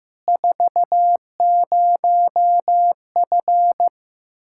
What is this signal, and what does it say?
Morse code, heard as a keyed tone, about 16 wpm, 700 Hz. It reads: 40F
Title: Practice QSO - Exchange 2 - Info - 15wpm